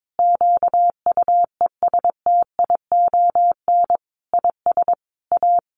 Morse code, heard as a keyed tone, about 22 wpm, 700 Hz. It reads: QUESTION IS A
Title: Sentences from Top 1200 Words 22wpm